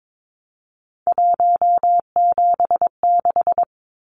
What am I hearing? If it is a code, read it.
Morse code, heard as a keyed tone, about 22 wpm, 700 Hz. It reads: 176
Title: Numbers - 3-Digit 22wpm